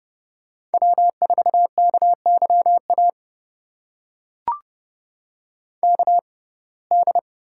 Morse code, heard as a keyed tone, about 30 wpm, spaced as for 22 wpm, 700 Hz. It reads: W4KYA KD
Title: USA Call Signs - 6x Character Spacing - 30wpm